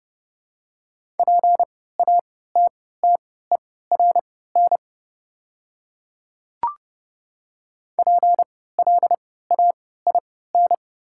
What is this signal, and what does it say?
Morse code, heard as a keyed tone, about 30 wpm, spaced as for 17 wpm, 700 Hz. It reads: PATTERN PLAIN
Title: 3x Character Spacing - Top 500 Words - 30wpm